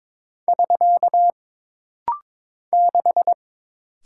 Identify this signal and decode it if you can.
Morse code, heard as a keyed tone, about 22 wpm, 700 Hz. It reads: <SK> 6
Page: This rapid-fire timing is geared to push your ICR (Instant Character Recognition) skills.